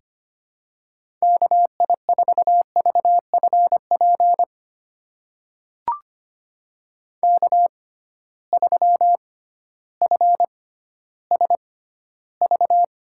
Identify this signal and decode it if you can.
Morse code, heard as a keyed tone, about 25 wpm, spaced as for 14 wpm, 700 Hz. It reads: KI4VFP K3FSV
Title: USA Call Signs - 6x Character Spacing - 25wpm